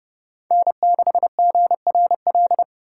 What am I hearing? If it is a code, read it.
Morse code, heard as a keyed tone, about 30 wpm, 700 Hz. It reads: N6GRL
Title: USA Call Signs - 5 Characters - 4x Character Spacing - 30wpm